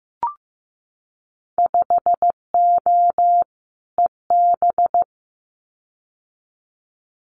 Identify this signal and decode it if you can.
Morse code, heard as a keyed tone, about 15 wpm, 700 Hz. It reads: 5O EB
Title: Winter Field Day Exchanges Encore 15wpm